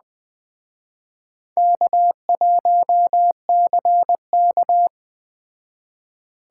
Morse code, heard as a keyed tone, about 20 wpm, 700 Hz. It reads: K1CK
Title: Speed Racing - USA 1x2 Call Signs 17wpm